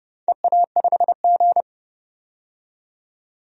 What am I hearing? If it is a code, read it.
Morse code, heard as a keyed tone, about 32 wpm, 700 Hz. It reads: EA5G
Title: DXCC-100 Call Signs - 4 Characters - 30wpm